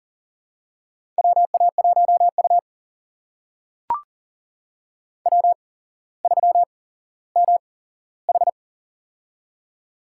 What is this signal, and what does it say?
Morse code, heard as a keyed tone, about 43 wpm, spaced as for 17 wpm, 700 Hz. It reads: WA1U W3MH
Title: USA Call Signs - 8x Character Spacing - 40wpm